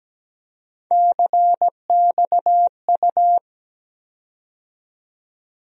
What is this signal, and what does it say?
Morse code, heard as a keyed tone, about 17 wpm, 700 Hz. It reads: CXU
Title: USA Call Sign Postfixes - 3 Letters 17wpm